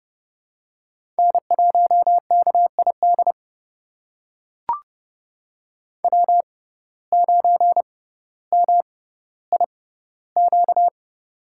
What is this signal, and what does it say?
Morse code, heard as a keyed tone, about 30 wpm, spaced as for 15 wpm, 700 Hz. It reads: N1KID W9MIQ
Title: USA Call Signs - 5 Characters - 6x Character Spacing - 30wpm